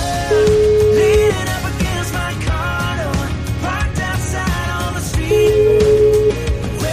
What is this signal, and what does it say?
Ringback tone (Europe, 425 Hz), with noise.